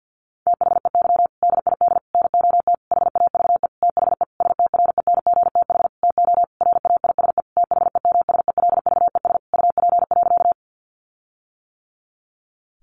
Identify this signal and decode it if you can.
Morse code, heard as a keyed tone, about 50 wpm, 700 Hz. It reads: THEY DID NOT HAVE THE STRENGTH TO RAISE THEMSELVES UP.
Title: Sentences from Top 1400 Words 50wpm